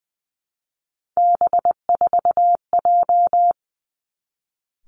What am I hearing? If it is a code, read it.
Morse code, heard as a keyed tone, about 19 wpm, 700 Hz. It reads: B4J
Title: Three Challenging Letter-Number Combinations 20wpm